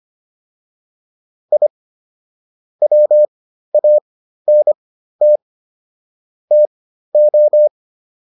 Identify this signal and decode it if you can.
Morse code, heard as a keyed tone, about 25 wpm, spaced as for 13 wpm, 600 Hz. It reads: I WANT TO